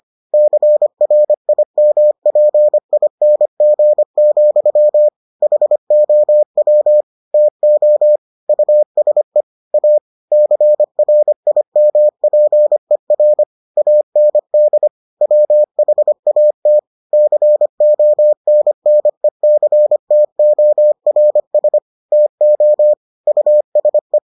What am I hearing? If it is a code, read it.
Morse code, heard as a keyed tone, about 25 wpm, 600 Hz. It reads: CRIMPING, HOW TO USE A CRIMPER AND WHAT CONNECTORS TO USE